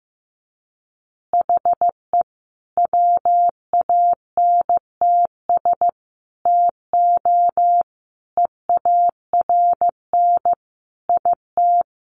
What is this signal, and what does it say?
Morse code, heard as a keyed tone, about 15 wpm, 700 Hz. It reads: HE WANTS TO EARN IT
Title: Additional Easy Phrases 15wpm